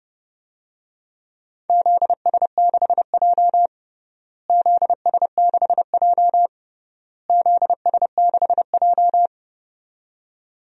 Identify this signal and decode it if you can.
Morse code, heard as a keyed tone, about 30 wpm, 700 Hz. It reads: ZS6J ZS6J ZS6J
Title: DXCC-100 Call Signs - 4 Characters - Repeated 3x - 30wpm